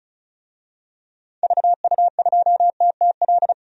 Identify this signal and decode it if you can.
Morse code, heard as a keyed tone, about 36 wpm, 700 Hz. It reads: VU2TTL
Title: DXCC-100 Call Signs - 6 Characters - 35wpm